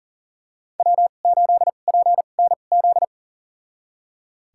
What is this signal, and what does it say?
Morse code, heard as a keyed tone, about 37 wpm, spaced as for 28 wpm, 700 Hz. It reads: W8PNZ
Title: USA Call Signs - 2x Character Spacing - 40wpm